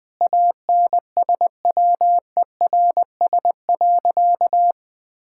Morse code, heard as a keyed tone, about 20 wpm, 700 Hz. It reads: ANSWERS.
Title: Sentences with 5-Word Phrases - 20wpm